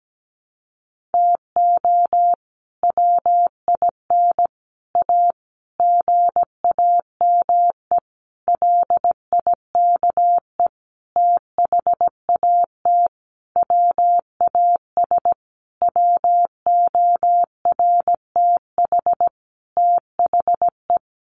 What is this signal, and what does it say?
Morse code, heard as a keyed tone, about 17 wpm, 700 Hz. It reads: TO WIN A GAME LIKE THAT WAS WORTH THE
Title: Sentences from Top 1500 Words 17wpm